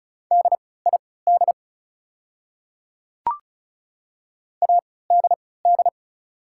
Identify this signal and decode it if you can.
Morse code, heard as a keyed tone, about 36 wpm, spaced as for 20 wpm, 700 Hz. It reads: DID ADD
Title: 3x Character Spacing - Top 200 Words - 35wpm